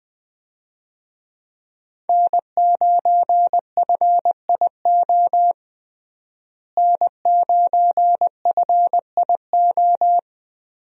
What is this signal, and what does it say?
Morse code, heard as a keyed tone, about 20 wpm, 700 Hz. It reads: N9FIO N9FIO
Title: USA Call Signs - 5 Characters - Repeated 3x - 20wpm